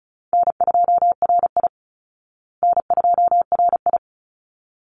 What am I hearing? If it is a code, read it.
Morse code, heard as a keyed tone, about 34 wpm, 700 Hz. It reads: N2RI N2RI
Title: Call Signs Repeated 3x - 35wpm